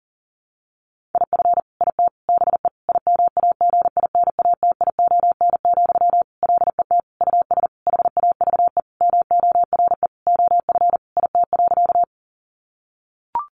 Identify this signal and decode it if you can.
Morse code, heard as a keyed tone, about 40 wpm, 700 Hz. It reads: IF IT BE IMAGINATION, LET US HAVE MORE OF IT.
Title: Sentences from Top 1300 Words 40wpm